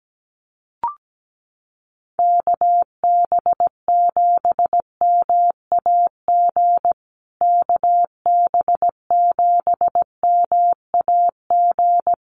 Morse code, heard as a keyed tone, about 17 wpm, 700 Hz. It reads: KB7MAG KB7MAG
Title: USA Call Signs Repeated 3x 17wpm